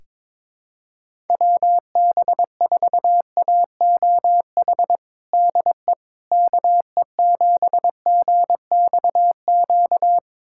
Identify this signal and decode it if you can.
Morse code, heard as a keyed tone, about 22 wpm, 700 Hz. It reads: WB4AOH DE KE7GXQ
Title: USA Callsign DE Callsign 22wpm